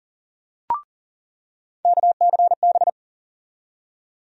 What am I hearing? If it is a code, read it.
Morse code, heard as a keyed tone, about 40 wpm, 700 Hz. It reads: KCB
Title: USA Call Sign Postfixes - 3 Letters 40wpm